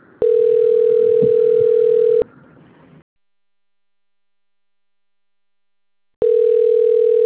Ringback tone (North America), 440 + 480 Hz.